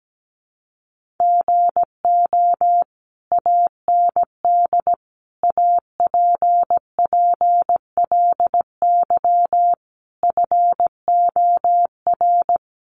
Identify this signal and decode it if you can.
Morse code, heard as a keyed tone, about 17 wpm, 700 Hz. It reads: GO AND APPLY FOR
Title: Sentences from Top 1200 Words 17wpm